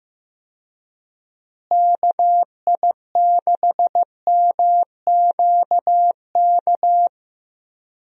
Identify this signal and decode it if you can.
Morse code, heard as a keyed tone, about 15 wpm, 700 Hz. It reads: KI6MQK
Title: USA Call Signs - 6 Characters - Repeated 3x - 15wpm